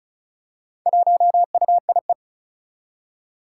Morse code, heard as a keyed tone, about 36 wpm, 700 Hz. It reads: 1UIE
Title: Four Challenging Letter-Number Combinations 35wpm